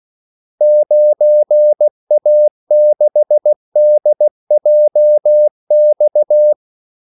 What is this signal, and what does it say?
Morse code, heard as a keyed tone, about 16 wpm, 600 Hz. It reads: 9A6DJX
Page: Kaldesignaler 16 wpm